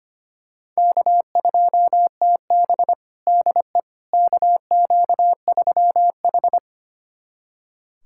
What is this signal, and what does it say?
Morse code, heard as a keyed tone, about 25 wpm, 700 Hz. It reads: K2TB DE KQ3H
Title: USA Callsign DE Callsign - 4 Characters - 25wpm